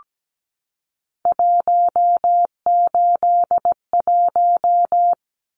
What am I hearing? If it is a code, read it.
Morse code, heard as a keyed tone, about 17 wpm, 700 Hz. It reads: 181
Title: Numbers - 3-Digit 17wpm